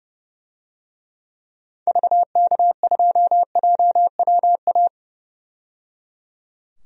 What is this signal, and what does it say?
Morse code, heard as a keyed tone, about 30 wpm, 700 Hz. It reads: VK2JWA
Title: DXCC-100 Call Signs - 6 Characters - 30wpm